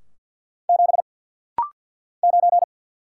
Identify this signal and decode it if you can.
Morse code, heard as a keyed tone, about 50 wpm, 700 Hz. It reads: 7 9
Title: Single Letter-Number Alt - Rapid-Fire 50wpm